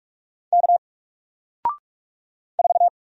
Morse code, heard as a keyed tone, about 44 wpm, 700 Hz. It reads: K 4
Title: Single Letter-Number Alt 45wpm